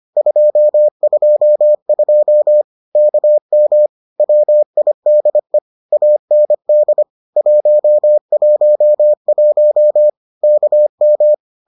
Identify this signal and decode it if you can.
Morse code, heard as a keyed tone, about 25 wpm, 600 Hz. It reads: 222 KM WIDE AND 111 KM